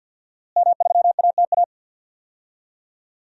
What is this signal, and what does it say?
Morse code, heard as a keyed tone, about 50 wpm, 700 Hz. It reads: M3ATA
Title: DXCC-100 Call Signs - 5 Characters - Repeated 3x - 50wpm